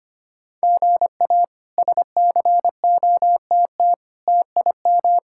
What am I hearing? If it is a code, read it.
Morse code, heard as a keyed tone, about 25 wpm, 700 Hz. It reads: GA SCOTT TIM
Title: K1USN SST QSO Practice - 25wpm